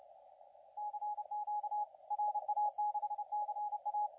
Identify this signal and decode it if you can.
Morse code, heard as a keyed tone, about 31 wpm, 840 Hz, with radio noise. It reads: CQ RA6CA
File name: la4c / sac-cw-2017 / CONTEST AUDIO 2100Z